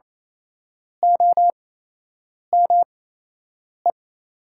Morse code, heard as a keyed tone, about 27 wpm, spaced as for 8 wpm, 700 Hz. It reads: OME
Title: 8x Character Spacing - Top 300 Words - 28wpm